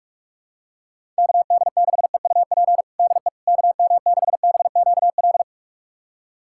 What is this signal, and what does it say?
Morse code, heard as a keyed tone, about 45 wpm, 700 Hz. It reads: KD6EUP DE KM6BQL
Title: USA Callsign DE Callsign - 6 Characters - 45wpm